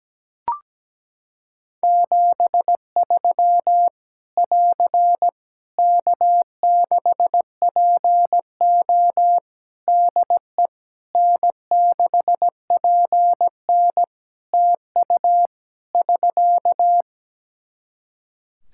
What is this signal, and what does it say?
Morse code, heard as a keyed tone, about 17 wpm, 700 Hz. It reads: 73 <AR> K6PO DE N6PN TU <SK>
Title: K1USN SST QSO Practice 17wpm